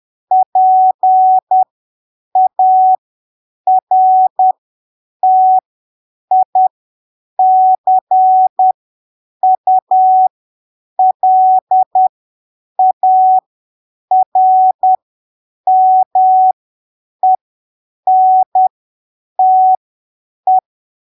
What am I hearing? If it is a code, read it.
Morse code, heard as a keyed tone, about 10 wpm, spaced as for 7 wpm, 750 Hz. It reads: PARTICULARMENTE
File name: arquivos / texto / texto 25 10ppm